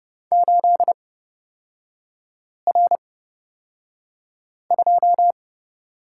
Morse code, heard as a keyed tone, about 29 wpm, spaced as for 5 wpm, 700 Hz. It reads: 8R2
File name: Lesson 38e - Letter q - Three Characters 30wpm